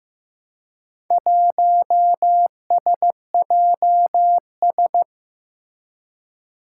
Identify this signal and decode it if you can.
Morse code, heard as a keyed tone, about 15 wpm, 700 Hz. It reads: 1SJS
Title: Four Challenging Letter-Number Combinations 15wpm